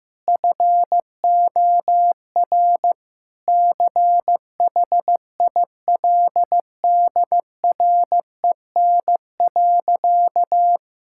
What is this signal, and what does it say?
Morse code, heard as a keyed tone, about 15 wpm, 700 Hz. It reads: FOR CHILDREN.
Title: Sentences from Top 1000 Words - Review Part 5 - 15wpm